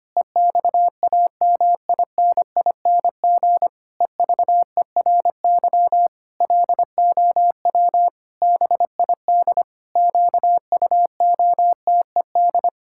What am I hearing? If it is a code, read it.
Morse code, heard as a keyed tone, about 25 wpm, 700 Hz. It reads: EXAMINING EVERY LOW BID QUOTED
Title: Pangram Challenge v2 - 25wpm